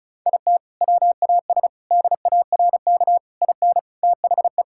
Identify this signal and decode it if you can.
Morse code, heard as a keyed tone, about 36 wpm, 700 Hz. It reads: IT WAS DARK IN THE
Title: Sentences from Top 1600 Words 35wpm